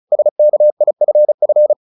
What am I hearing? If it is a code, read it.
Morse code, heard as a keyed tone, about 34 wpm, 600 Hz. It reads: SKIFF
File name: the old man and the sea 2 35wpm